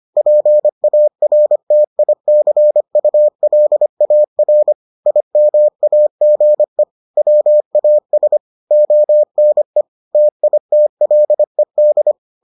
Morse code, heard as a keyed tone, about 25 wpm, 600 Hz. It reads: PARTICULAR IMAGE WAS ONE TITLED